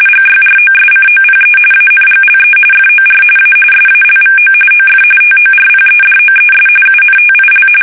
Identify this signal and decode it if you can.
RTTY, about 45.45 baud. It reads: GYROSCOPE ANGLE FOR A SALVO OF TORPEDOES UNTIL
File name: fsk850